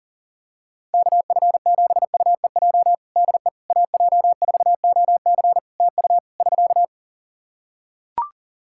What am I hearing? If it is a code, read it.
Morse code, heard as a keyed tone, about 40 wpm, 700 Hz. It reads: KF7UEJ DE AJ4OC TU <SK>
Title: Practice QSO - Exchange 4 - Ending - 40wpm